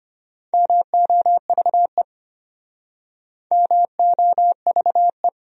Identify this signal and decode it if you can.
Morse code, heard as a keyed tone, about 36 wpm, 700 Hz. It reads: MOVE MOVE